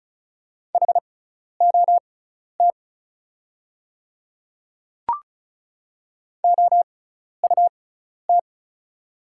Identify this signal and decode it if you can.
Morse code, heard as a keyed tone, about 36 wpm, spaced as for 12 wpm, 700 Hz. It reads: HOT OUT
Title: 6x Character Spacing - Top 500 Words - 35wpm